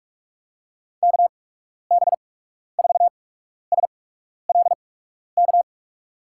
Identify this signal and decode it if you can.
Morse code, heard as a keyed tone, about 44 wpm, spaced as for 12 wpm, 700 Hz. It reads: KB4SLK
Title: USA Call Signs - 6 Characters - 8x Character Spacing - 45wpm